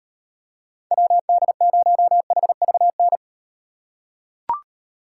Morse code, heard as a keyed tone, about 37 wpm, 700 Hz. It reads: WD0HVN